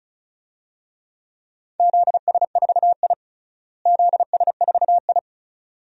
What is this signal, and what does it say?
Morse code, heard as a keyed tone, about 34 wpm, 700 Hz. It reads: ZS4I ZS4I
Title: DXCC-100 Call Signs - 4 Characters - Repeated 2x - 35wpm